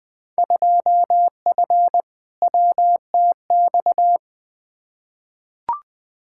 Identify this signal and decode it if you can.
Morse code, heard as a keyed tone, about 20 wpm, 700 Hz. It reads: 2F WTX
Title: Field Day Exchanges - 20wpm